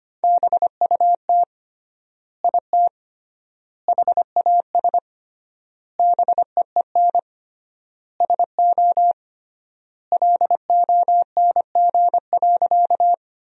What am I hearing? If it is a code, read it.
Morse code, heard as a keyed tone, about 25 wpm, 700 Hz. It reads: BUT IT HAS BEEN SO LONG.
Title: 3x Word-Spacing - Sentences from Top 100 Words - 25wpm